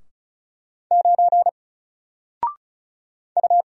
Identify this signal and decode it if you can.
Morse code, heard as a keyed tone, about 36 wpm, 700 Hz. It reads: 9 U
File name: Single-Letter-Number-35wpm